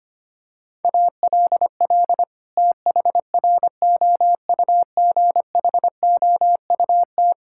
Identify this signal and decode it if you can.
Morse code, heard as a keyed tone, about 25 wpm, 700 Hz. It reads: ALL THROUGHOUT
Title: Sentences from Top 1000 Words - Review Part 5 - 25wpm